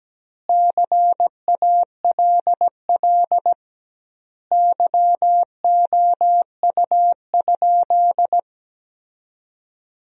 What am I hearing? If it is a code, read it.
Morse code, heard as a keyed tone, about 17 wpm, 700 Hz. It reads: CALL YOU?
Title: 2x Word-Spacing - Sentences from Top 100 Words - 17wpm